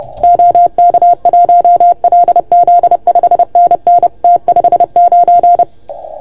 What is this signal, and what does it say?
Morse code, heard as a keyed tone, about 30 wpm, 690 Hz. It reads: OK1LZ 5NN T59
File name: Archiv / CUC381 / OK1FGD CUC 381